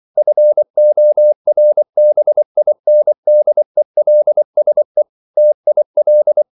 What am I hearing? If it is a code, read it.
Morse code, heard as a keyed tone, about 24 wpm, 600 Hz. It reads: FORBINDELSE TIL
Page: Ferie 24 wpm